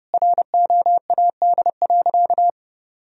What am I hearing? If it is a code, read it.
Morse code, heard as a keyed tone, about 30 wpm, 700 Hz. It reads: ROAD.
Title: Sentences from Top 1000 Words - Review Part 2 - 30wpm